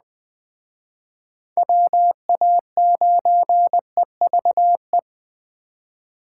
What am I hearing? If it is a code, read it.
Morse code, heard as a keyed tone, about 20 wpm, 700 Hz. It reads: WA9EVE